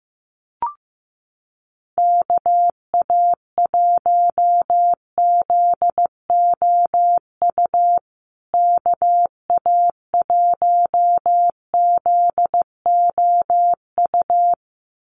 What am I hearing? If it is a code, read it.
Morse code, heard as a keyed tone, about 15 wpm, 700 Hz. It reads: KA1ZOU KA1ZOU
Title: USA Call Signs Repeated 2x 15wpm